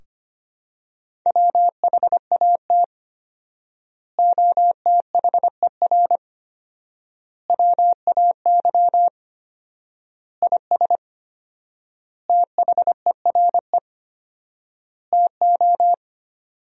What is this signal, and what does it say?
Morse code, heard as a keyed tone, about 25 wpm, 700 Hz. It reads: WHAT OTHER WAY IS THERE TO
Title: 4x Word-Spacing - Sentences from Top 100 Words - 25wpm